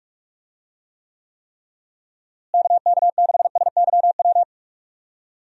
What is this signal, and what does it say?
Morse code, heard as a keyed tone, about 44 wpm, 700 Hz. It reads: KK6SYW
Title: USA Call Signs - 6 Characters - Repeated 3x - 45wpm